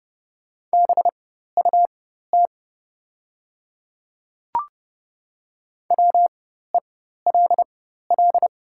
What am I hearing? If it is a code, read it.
Morse code, heard as a keyed tone, about 30 wpm, spaced as for 14 wpm, 700 Hz. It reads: BUT WELL
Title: 4x Character Spacing - Top 300 Words - 30wpm